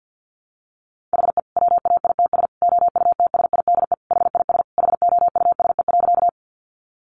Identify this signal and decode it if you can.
Morse code, heard as a keyed tone, about 50 wpm, 700 Hz. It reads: HE WAITS OUTSIDE HIS HOUSE.
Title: Additional Easy Phrases 50wpm